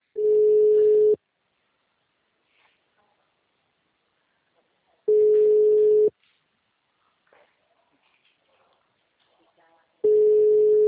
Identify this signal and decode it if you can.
Ringback tone (Europe, 425 Hz), with speech.